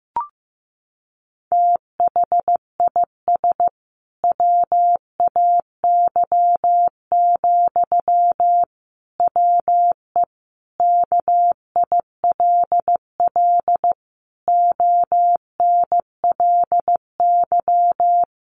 Morse code, heard as a keyed tone, about 14 wpm, 700 Hz. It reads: THIS WAY, WE KILL ONLY
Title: A Sound Of Thunder 15wpm